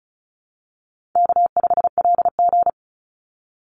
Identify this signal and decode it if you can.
Morse code, heard as a keyed tone, about 34 wpm, 700 Hz. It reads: K5LG
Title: Easy Call Signs Repeated 2x - 35wpm